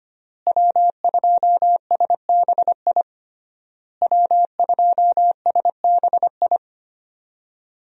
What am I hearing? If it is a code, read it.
Morse code, heard as a keyed tone, about 25 wpm, 700 Hz. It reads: W2SBI W2SBI
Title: USA Call Signs - 5 Characters - Repeated 3x - 25wpm